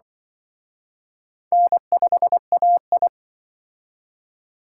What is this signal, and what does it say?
Morse code, heard as a keyed tone, about 24 wpm, 700 Hz. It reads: N5AI
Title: Speed Racing - USA 1x2 Call Signs 20wpm